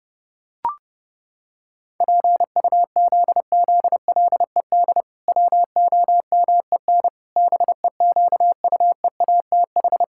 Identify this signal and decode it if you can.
Morse code, heard as a keyed tone, about 32 wpm, 700 Hz. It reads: PUZZLED WOMEN BEQUEATH
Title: Pangram Challenge v2 - 30wpm